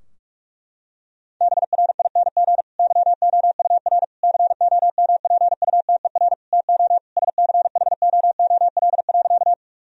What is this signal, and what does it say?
Morse code, heard as a keyed tone, about 48 wpm, 700 Hz. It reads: BRING YOUR COMPUTER TO SCHOOL.
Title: Sentences from Top 1000 Words - Review Part 3 - 45wpm